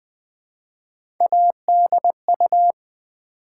Morse code, heard as a keyed tone, about 20 wpm, 700 Hz. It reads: ADU
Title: Build A Word - 1 Letter At A Time - 5-Letter Words - 20wpm